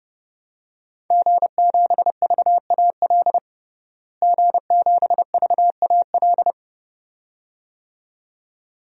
Morse code, heard as a keyed tone, about 30 wpm, 700 Hz. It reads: G7VAL G7VAL
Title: DXCC-100 Call Signs - 5 Characters - Repeated 2x - 30wpm